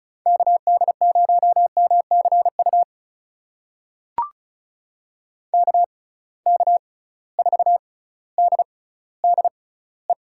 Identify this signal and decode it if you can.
Morse code, heard as a keyed tone, about 36 wpm, spaced as for 18 wpm, 690 Hz. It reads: KD0MCU KK4DDE
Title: USA Call Signs - 6 Characters - 6x Character Spacing - 35wpm